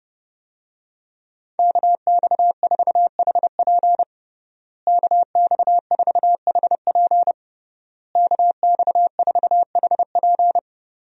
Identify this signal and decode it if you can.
Morse code, heard as a keyed tone, about 30 wpm, 700 Hz. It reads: KX4HP KX4HP KX4HP
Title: USA Call Signs - 5 Characters - Repeated 3x - 30wpm